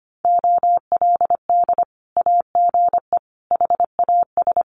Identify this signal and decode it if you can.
Morse code, heard as a keyed tone, about 25 wpm, 700 Hz. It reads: OLD AGE HAS
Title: Sentences from Top 1100 Words 25wpm